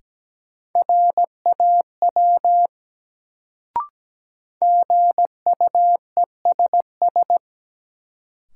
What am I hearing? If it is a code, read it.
Morse code, heard as a keyed tone, about 17 wpm, 700 Hz. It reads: RAW GUESS
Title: Sentences from Top 1500 Words - New Word Drill Alt - 17wpm